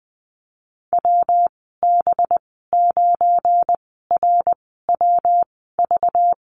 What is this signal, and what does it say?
Morse code, heard as a keyed tone, about 20 wpm, spaced as for 14 wpm, 700 Hz. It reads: WB9RWV
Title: 193 Lesson 31g - Letter v - Call Signs 20wpm